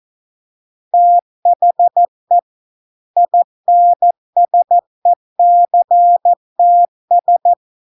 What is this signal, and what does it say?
Morse code, heard as a keyed tone, about 14 wpm, 700 Hz. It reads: THE INSECTS